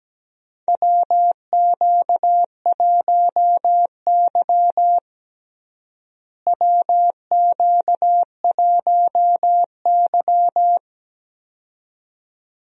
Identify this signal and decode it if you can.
Morse code, heard as a keyed tone, about 17 wpm, 700 Hz. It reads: WQ1Y WQ1Y
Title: USA Call Signs - 4 Characters - Repeated 2x - 17wpm